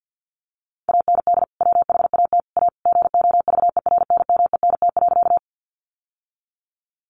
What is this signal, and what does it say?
Morse code, heard as a keyed tone, about 50 wpm, 700 Hz. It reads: AND WHAT A GOVERNMENT.
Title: Sentences from Top 600 Words 50wpm